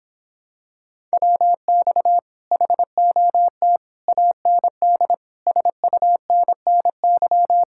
Morse code, heard as a keyed tone, about 26 wpm, 700 Hz. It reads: WX HOT AND SUNNY